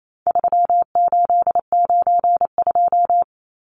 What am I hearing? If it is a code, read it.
Morse code, heard as a keyed tone, about 28 wpm, 700 Hz. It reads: 3892
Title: Numbers - 4-Digit 28wpm